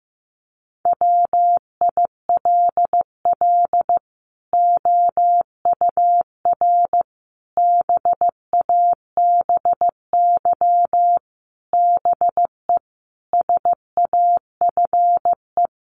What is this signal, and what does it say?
Morse code, heard as a keyed tone, about 15 wpm, 700 Hz. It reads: WILL OUR BABY BE SAFE
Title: Sentences from Top 1300 Words 15wpm